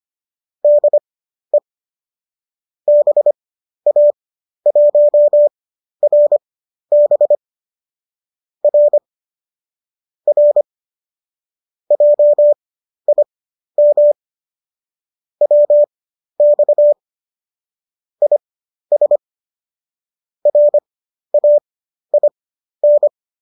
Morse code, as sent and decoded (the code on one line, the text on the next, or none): -.. / -... .- .---- .-. -... / .-. / .-. / .--- .. -- / .-- -..- / .. ... / .-. .- .. -.
DE BA1RB R R JIM WX IS RAIN